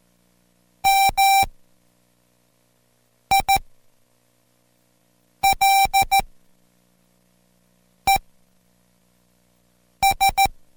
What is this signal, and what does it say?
Morse code, heard as a keyed tone, about 14 wpm, spaced as for 4 wpm, 780 Hz. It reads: MILES